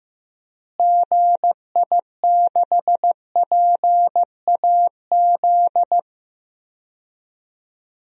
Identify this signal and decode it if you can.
Morse code, heard as a keyed tone, about 15 wpm, 700 Hz. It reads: GI6PAZ
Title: DXCC-100 Call Signs - 6 Characters - Repeated 3x - 15wpm